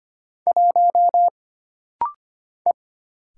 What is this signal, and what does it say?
Morse code, heard as a keyed tone, about 25 wpm, 700 Hz. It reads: 1 E
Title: Single Letter-Number - Rapid-Fire 25wpm